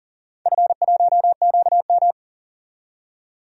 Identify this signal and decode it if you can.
Morse code, heard as a keyed tone, about 40 wpm, 700 Hz. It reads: F1QM
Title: DXCC-100 Call Signs - 4 Characters - 40wpm